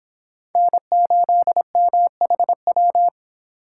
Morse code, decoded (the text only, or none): N8MHW